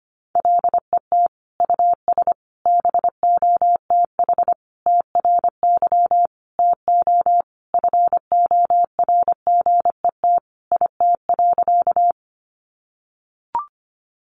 Morse code, heard as a keyed tone, about 25 wpm, 700 Hz. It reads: LET US BOTH TRY TO FORGET IT.
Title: Sentences from Top 1200 Words 25wpm